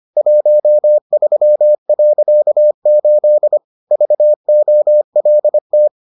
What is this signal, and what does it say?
Morse code, heard as a keyed tone, about 25 wpm, 600 Hz. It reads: 13.8 VOLT